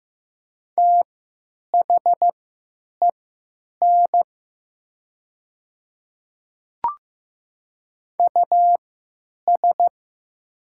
Morse code, heard as a keyed tone, about 15 wpm, spaced as for 9 wpm, 700 Hz. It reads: THEN US
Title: 3x Character Spacing - Top 300 Words - 15wpm